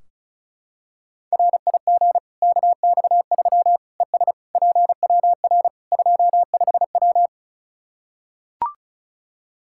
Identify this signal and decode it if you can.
Morse code, heard as a keyed tone, about 34 wpm, 700 Hz. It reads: RIG KX3 ES PWR 25W
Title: K1USN SST QSO Practice 35wpm